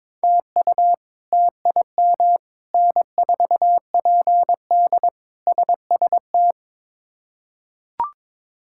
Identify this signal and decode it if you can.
Morse code, heard as a keyed tone, about 21 wpm, 700 Hz. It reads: TU TIM N4PD SST
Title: K1USN SST QSO Practice - 22wpm